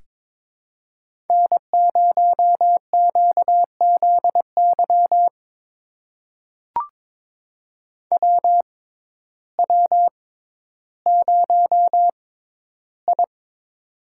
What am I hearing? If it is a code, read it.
Morse code, heard as a keyed tone, about 23 wpm, spaced as for 12 wpm, 700 Hz. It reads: N0QZY WW0I
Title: USA Call Signs - 6x Character Spacing - 22wpm